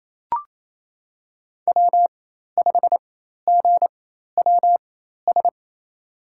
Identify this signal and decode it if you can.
Morse code, heard as a keyed tone, about 28 wpm, spaced as for 13 wpm, 700 Hz. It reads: W5GWS
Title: USA Call Signs - 5 Characters - 4x Character Spacing - 28wpm